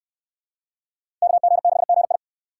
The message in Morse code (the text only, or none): KG6GI